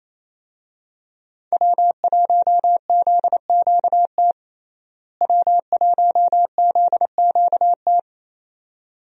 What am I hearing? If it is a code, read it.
Morse code, heard as a keyed tone, about 27 wpm, 700 Hz. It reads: W1ZQT W1ZQT
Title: USA Call Signs - 5 Characters - Repeated 2x - 28wpm